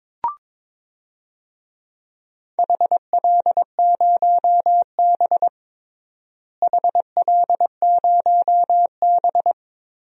Morse code, heard as a keyed tone, about 22 wpm, 700 Hz. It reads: HL0B HL0B
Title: DXCC-100 Call Signs - 4 Characters - Repeated 2x - 22wpm